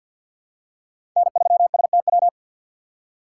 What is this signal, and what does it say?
Morse code, heard as a keyed tone, about 50 wpm, 700 Hz. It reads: N3STW
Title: USA Call Signs - 6x Character Spacing - 50wpm